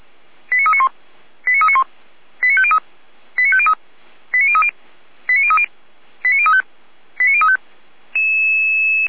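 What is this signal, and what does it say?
Five-tone selective call (ZVEI), heard as quick runs of five tones, tones of about 70 ms. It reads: 89391 89391 89503 89503 89030 89030 89035 89035
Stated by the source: ZVEI five-tone selective call received on SDR.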